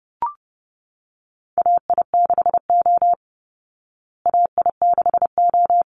Morse code, heard as a keyed tone, about 30 wpm, 700 Hz. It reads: AI6O AI6O
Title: Call Signs Repeated 2x - 30wpm